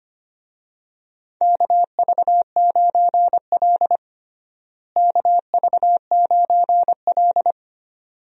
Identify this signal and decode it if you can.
Morse code, heard as a keyed tone, about 25 wpm, 700 Hz. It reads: KV9L KV9L
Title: USA Call Signs - 4 Characters - Repeated 2x - 25wpm